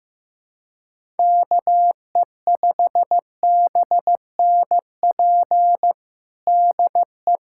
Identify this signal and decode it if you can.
Morse code, heard as a keyed tone, about 15 wpm, 700 Hz. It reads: KE5BNP DE
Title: USA Callsign DE Callsign 15wpm